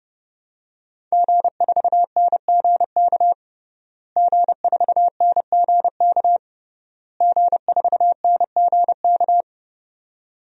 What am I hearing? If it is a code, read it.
Morse code, heard as a keyed tone, about 32 wpm, 700 Hz. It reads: G4NGK G4NGK G4NGK
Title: DXCC-100 Call Signs - 5 Characters - Repeated 3x - 30wpm